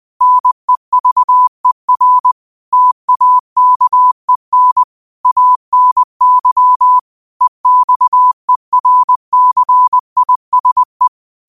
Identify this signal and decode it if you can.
Morse code, heard as a keyed tone, about 20 wpm, 1000 Hz. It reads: NEVER TAKEN ANY EXERCISE